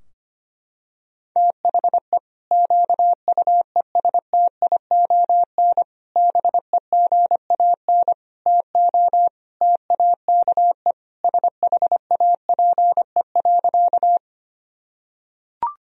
Morse code, heard as a keyed tone, about 25 wpm, 700 Hz. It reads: THE QUESTION BEGAN TO TAKE SHAPE.
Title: Sentences from Top 1000 Words - Review Part 3 - 25wpm